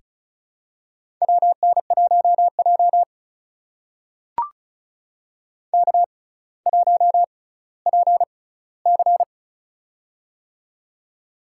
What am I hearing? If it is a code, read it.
Morse code, heard as a keyed tone, about 36 wpm, spaced as for 18 wpm, 700 Hz. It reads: WN1J K1PC
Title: USA Call Signs - 6x Character Spacing - 35wpm